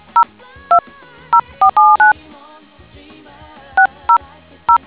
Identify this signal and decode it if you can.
DTMF, heard as telephone touch tones, tones of about 100 ms. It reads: 020479 600